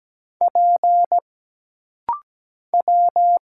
Morse code, heard as a keyed tone, about 17 wpm, 700 Hz. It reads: P W
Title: Single Letters - Rapid-Fire 17wpm